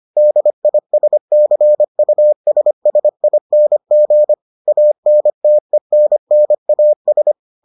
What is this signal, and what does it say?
Morse code, heard as a keyed tone, about 25 wpm, 600 Hz. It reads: DISCUSSING ANTENNAS